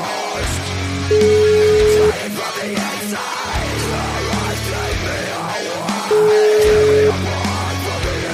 Ringback tone (Europe, 425 Hz), with noise.